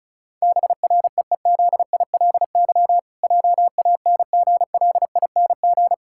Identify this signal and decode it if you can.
Morse code, heard as a keyed tone, about 34 wpm, 700 Hz. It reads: BREEZILY JANGLING
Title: Pangram Challenge v2 - 35wpm